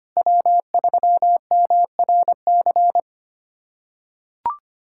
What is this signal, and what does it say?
Morse code, heard as a keyed tone, about 25 wpm, 700 Hz. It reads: W3MRC
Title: USA Callsign DE Callsign - 5 Characters - 25wpm